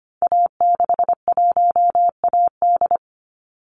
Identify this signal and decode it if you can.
Morse code, heard as a keyed tone, about 25 wpm, 700 Hz. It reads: A61AD
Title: Call Signs Repeated 2x - 25wpm